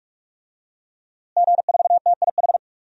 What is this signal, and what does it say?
Morse code, heard as a keyed tone, about 46 wpm, 700 Hz. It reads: G4TIH
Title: DXCC-100 Call Signs - 5 Characters - Repeated 2x - 45wpm